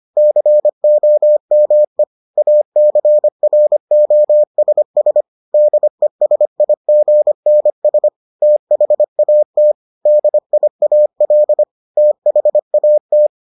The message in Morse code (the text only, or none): COME ACROSS DESIGNS THAT DIAL THAT